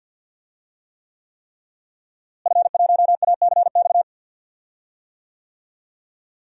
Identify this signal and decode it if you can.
Morse code, heard as a keyed tone, about 49 wpm, 700 Hz. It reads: F1ACX
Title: DXCC-100 Call Signs - 5 Characters - Repeated 3x - 50wpm